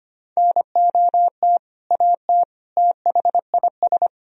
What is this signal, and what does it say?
Morse code, heard as a keyed tone, about 25 wpm, 700 Hz. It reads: NOT AT THIS
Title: Sentences from Top 1000 Words - Review Part 4 - 25wpm